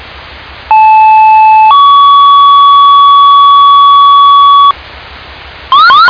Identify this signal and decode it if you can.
Two-tone page, heard as a one-second tone, then a three-second tone, about 870 then 1130 Hz, with noise. It is heard from 0.7 to 4.7 s.